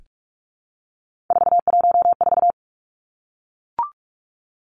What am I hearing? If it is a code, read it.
Morse code, heard as a keyed tone, about 44 wpm, 700 Hz. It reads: 414
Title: Numbers - 3-Digit 45wpm